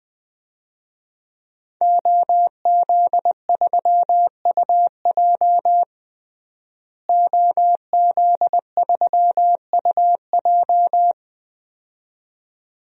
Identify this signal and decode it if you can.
Morse code, heard as a keyed tone, about 20 wpm, 700 Hz. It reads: OZ3UJ OZ3UJ
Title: DXCC-100 Call Signs - 5 Characters - Repeated 2x - 20wpm